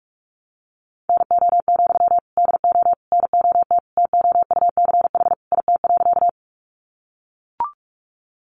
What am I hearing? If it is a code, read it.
Morse code, heard as a keyed tone, about 44 wpm, 700 Hz. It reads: NO, DO NOT TOUCH IT.
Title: Sentences from Top 1200 Words 45wpm